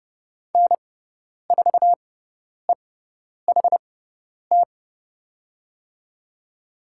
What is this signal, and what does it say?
Morse code, heard as a keyed tone, about 30 wpm, spaced as for 10 wpm, 700 Hz. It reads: N4EHT
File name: Lesson 16g - Letter c - Call Signs 30wpm